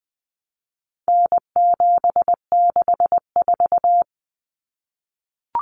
Morse code, heard as a keyed tone, about 20 wpm, 700 Hz. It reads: N764
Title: Common Cut Numbers - 4-Digit 20wpm